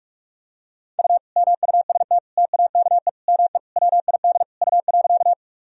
Morse code, heard as a keyed tone, about 46 wpm, 700 Hz. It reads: U MUST TAKE ME WID U.
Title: Sentences from Top 200 Words and Abbreviations 45wpm